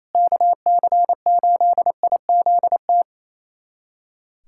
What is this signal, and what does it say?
Morse code, heard as a keyed tone, about 28 wpm, 700 Hz. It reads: KC8IZT
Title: USA Callsign DE Callsign 28wpm